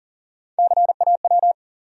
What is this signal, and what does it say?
Morse code, heard as a keyed tone, about 40 wpm, 700 Hz. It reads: CAW
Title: USA Call Sign Postfixes - 3 Letters 40wpm